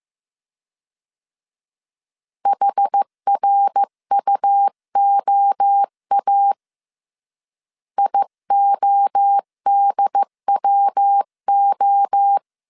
Morse code, heard as a keyed tone, about 15 wpm, 800 Hz. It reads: HRUOA IODWO
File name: CW / DET / 1 3 1500C S A 14wpm 2447